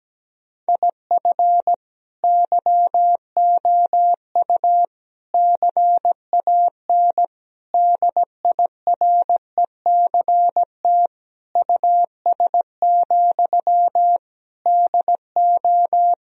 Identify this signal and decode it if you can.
Morse code, heard as a keyed tone, about 17 wpm, 700 Hz. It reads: IF YOU CAN DIRECT US, DO
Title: Sentences from Top 1000 Words - Review Part 2 - 17wpm